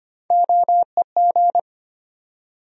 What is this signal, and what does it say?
Morse code, heard as a keyed tone, about 25 wpm, 700 Hz. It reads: OEG
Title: USA Call Sign Postfixes - 3 Letters 25wpm